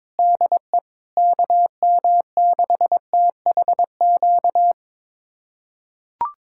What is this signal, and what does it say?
Morse code, heard as a keyed tone, about 22 wpm, 700 Hz. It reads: DE KM6THQ
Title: USA Callsign DE Callsign - 6 Characters - 22wpm